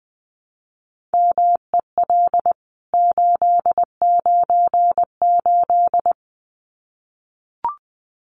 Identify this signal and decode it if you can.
Morse code, heard as a keyed tone, about 20 wpm, 700 Hz. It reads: MEL 898
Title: CWT Exchange 20wpm